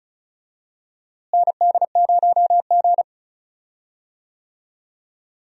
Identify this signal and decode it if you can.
Morse code, heard as a keyed tone, about 36 wpm, 700 Hz. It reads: ND0G
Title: USA Call Signs - 4 Characters - Repeated 2x - 35wpm